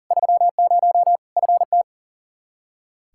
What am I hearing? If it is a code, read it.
Morse code, heard as a keyed tone, about 40 wpm, 700 Hz. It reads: 30 FT
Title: Practice QSO - Exchange 2 - Info - 40wpm